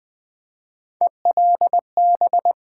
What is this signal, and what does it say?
Morse code, heard as a keyed tone, about 20 wpm, 700 Hz. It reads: ELB
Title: Build A Word - 1 Letter At A Time - 5-Letter Words - 20wpm